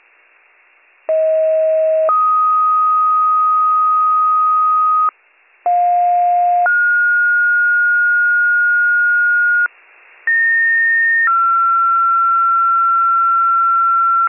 Two-tone page, heard as a one-second tone, then a three-second tone, about 640 then 1250 Hz, about 710 then 1430 Hz, about 1850 then 1360 Hz. It is heard from 1.1 to 14.3 s.